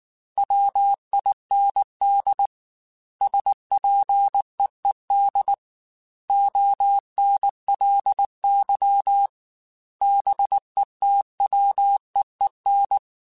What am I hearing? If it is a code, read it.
Morse code, heard as a keyed tone, about 19 wpm, 800 Hz. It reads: WIND SPEED ONLY BETWEEN
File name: Weather Sets / Weather Sets-mp3s / Weather 4 28 @18